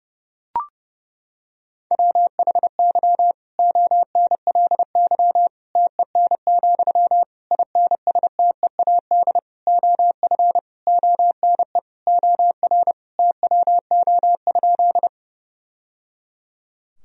Morse code, heard as a keyed tone, about 30 wpm, 700 Hz. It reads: WHY ONLY TEN, INSTEAD OF ONE OR TWO?
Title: Sentences from Top 1000 Words - Review Part 4 - 30wpm